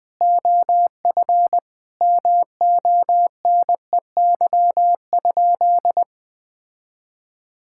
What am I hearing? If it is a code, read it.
Morse code, heard as a keyed tone, about 21 wpm, 700 Hz. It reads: OF MONEY?
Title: Sentences from Top 1000 Words - Review Part 5 - 20wpm